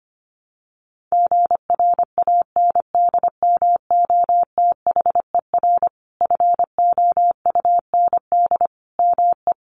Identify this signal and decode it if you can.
Morse code, heard as a keyed tone, about 25 wpm, 700 Hz. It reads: GRANDMOTHER FOUND ME
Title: Sentences from Top 1200 Words 25wpm